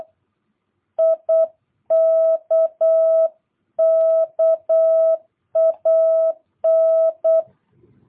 Morse code, heard as a keyed tone, about 8 wpm, 650 Hz. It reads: IKKAN
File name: jh8cbh / 2022cw